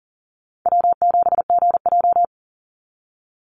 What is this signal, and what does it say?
Morse code, heard as a keyed tone, about 40 wpm, 700 Hz. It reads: W7GJ
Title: NA SOTA Chaser Callsigns 40wpm